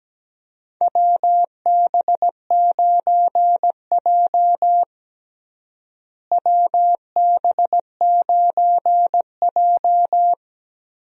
Morse code, heard as a keyed tone, about 17 wpm, 700 Hz. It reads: WB9J WB9J
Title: USA Call Signs - 4 Characters - Repeated 3x - 17wpm